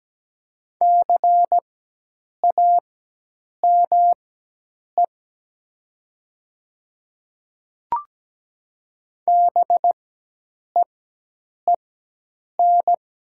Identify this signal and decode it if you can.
Morse code, heard as a keyed tone, about 17 wpm, spaced as for 8 wpm, 700 Hz. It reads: CAME BEEN
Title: 4x Character Spacing - Top 200 Words - 17wpm